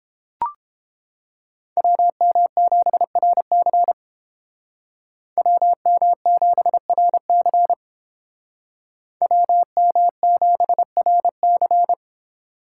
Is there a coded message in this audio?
Yes: Morse code from 1.8 to 11.9 s, heard as a keyed tone, about 30 wpm, 700 Hz. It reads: WM7RC WM7RC WM7RC